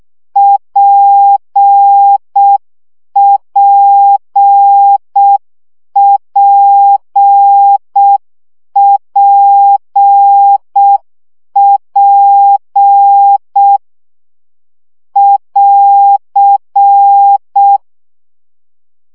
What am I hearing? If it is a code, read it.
Morse code, heard as a keyed tone, about 6 wpm, 800 Hz. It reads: PPPPP <AR>
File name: CW gyakorlas p 5 baud